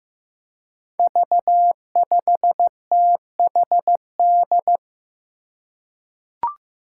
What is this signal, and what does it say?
Morse code, heard as a keyed tone, about 15 wpm, 700 Hz. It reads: V5THD
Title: Five Challenging Letter-Number Combinations 15wpm